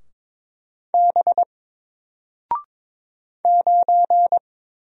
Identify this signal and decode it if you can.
Morse code, heard as a keyed tone, about 22 wpm, 700 Hz. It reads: B 9
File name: Single-Letter-Number-22wpm